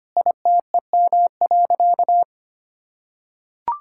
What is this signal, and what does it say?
Morse code, heard as a keyed tone, about 25 wpm, 700 Hz. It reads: ITEM.
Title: Sentences from Top 1000 Words - Review Part 6 - 25wpm